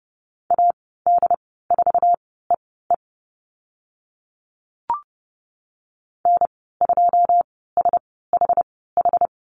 Morse code, heard as a keyed tone, about 30 wpm, spaced as for 17 wpm, 700 Hz. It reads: AD4EE N2SHH
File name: Lesson 14g - Number 2 - Call Signs 30wpm